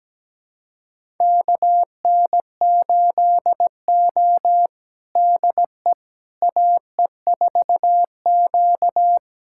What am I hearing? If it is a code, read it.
Morse code, heard as a keyed tone, about 17 wpm, 700 Hz. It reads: KN8O DE AE4Q
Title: USA Callsign DE Callsign - 4 Characters - 17wpm